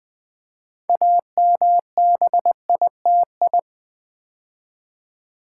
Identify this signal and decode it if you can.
Morse code, heard as a keyed tone, about 21 wpm, 700 Hz. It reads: AMBITI
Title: Build A Word - 3 Letters At A Time - 9-Letter Words - 20wpm